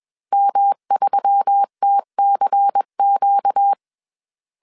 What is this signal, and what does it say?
Morse code, heard as a keyed tone, about 21 wpm, 800 Hz. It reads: M3TCQ
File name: CW / DET / 1 6 1500C S A 20wpm 2048